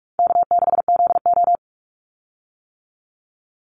Morse code, heard as a keyed tone, about 44 wpm, 700 Hz. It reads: K6ZO
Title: Call Signs - 45wpm